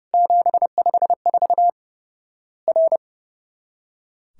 Morse code, heard as a keyed tone, about 30 wpm, 700 Hz. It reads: 754 R
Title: Worked All Europe DX Contest - QTC Exchange - 30wpm